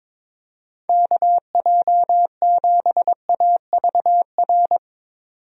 Morse code, heard as a keyed tone, about 22 wpm, 700 Hz. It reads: KJ7AVR
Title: USA Call Signs - 6 Characters - 22wpm